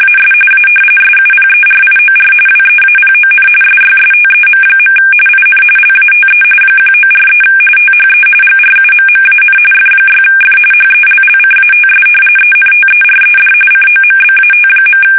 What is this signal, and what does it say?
RTTY, about 45.45 baud. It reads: AS A SUPERIOR WEAPON SYSTEM. IN LATE 1943 THE GRUMMAN F6F HELLCATS ENTERED COMBAT.
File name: fsk850